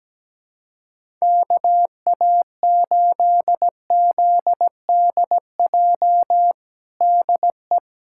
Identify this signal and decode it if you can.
Morse code, heard as a keyed tone, about 17 wpm, 700 Hz. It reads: KA8ZDJ DE
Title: USA Callsign DE Callsign 17wpm